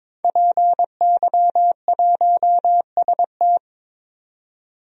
Morse code, heard as a keyed tone, about 22 wpm, 700 Hz. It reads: PY1ST
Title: DXCC-100 Call Signs - 5 Characters - 22wpm